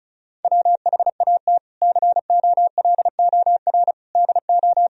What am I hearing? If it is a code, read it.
Morse code, heard as a keyed tone, about 34 wpm, 700 Hz. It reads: WHAT COLOR DO
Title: Sentences from Top 1000 Words - Review Part 2 - 35wpm